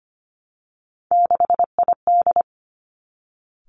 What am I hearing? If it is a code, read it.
Morse code, heard as a keyed tone, about 25 wpm, 700 Hz. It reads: 6ID
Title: Three Challenging Letter-Number Combinations 25wpm